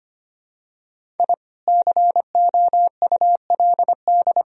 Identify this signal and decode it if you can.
Morse code, heard as a keyed tone, about 25 wpm, 700 Hz. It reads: I COULD
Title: Sentences from Top 1000 Words - Review Part 1 - 25wpm